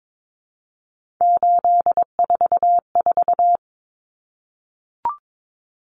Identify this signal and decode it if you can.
Morse code, heard as a keyed tone, about 22 wpm, 700 Hz. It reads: 844
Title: Numbers - 3-Digit 22wpm